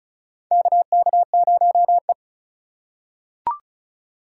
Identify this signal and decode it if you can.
Morse code, heard as a keyed tone, about 34 wpm, 700 Hz. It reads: KK0E
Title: USA Call Signs - 4 Characters - 3x Character Spacing - 35wpm